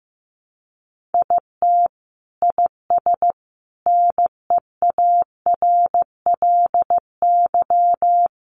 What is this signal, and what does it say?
Morse code, heard as a keyed tone, about 15 wpm, 700 Hz. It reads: IT IS NEARLY
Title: Sentences from Top 1300 Words 15wpm